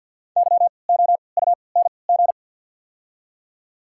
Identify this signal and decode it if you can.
Morse code, heard as a keyed tone, about 52 wpm, spaced as for 29 wpm, 700 Hz. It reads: YOUNG
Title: 3x Character Spacing - Top 500 Words - 50wpm